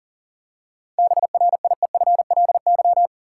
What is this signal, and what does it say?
Morse code, heard as a keyed tone, about 40 wpm, 700 Hz. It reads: BRIEFLY
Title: Seven Letter Words - 40wpm